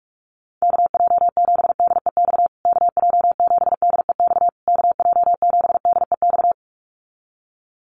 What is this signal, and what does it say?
Morse code, heard as a keyed tone, about 44 wpm, 700 Hz. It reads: KJ7DEX KJ7DEX KJ7DEX
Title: USA Call Signs Repeated 3x 45wpm